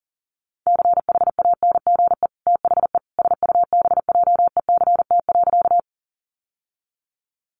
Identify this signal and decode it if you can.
Morse code, heard as a keyed tone, about 40 wpm, 700 Hz. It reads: CHANGE THE SUBJECT.
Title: Sentences from Top 800 Words 40wpm